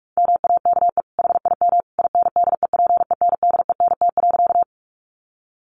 Morse code, heard as a keyed tone, about 44 wpm, 700 Hz. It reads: MAKE HIM INDEPENDENT.
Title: Sentences from Top 900 Words 45wpm